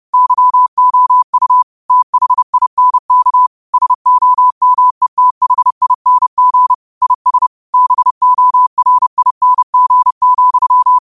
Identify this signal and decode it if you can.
Morse code, heard as a keyed tone, about 30 wpm, 1000 Hz. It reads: YOU THINK SOMETHING IS BORING,
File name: qotdfiles / 30wpm